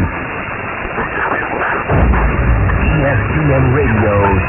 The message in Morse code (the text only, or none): none